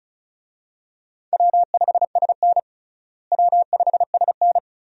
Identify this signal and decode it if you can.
Morse code, heard as a keyed tone, about 36 wpm, 700 Hz. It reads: W5SN W5SN
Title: USA Call Signs - 4 Characters - Repeated 2x - 35wpm